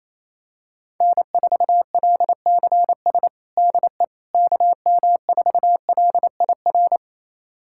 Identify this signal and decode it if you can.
Morse code, heard as a keyed tone, about 28 wpm, 700 Hz. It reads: N4LCS DE KM4LIR
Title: USA Callsign DE Callsign 28wpm